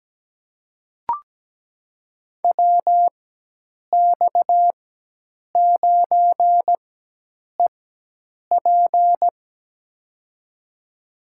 Morse code, heard as a keyed tone, about 17 wpm, spaced as for 8 wpm, 700 Hz. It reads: WX9EP
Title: USA Call Signs - 5 Characters - 4x Character Spacing - 17wpm